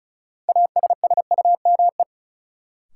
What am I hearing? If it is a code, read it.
Morse code, heard as a keyed tone, about 36 wpm, 700 Hz. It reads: ASSUME
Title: Sentences from Top 1500 Words - New Word Drill - 35wpm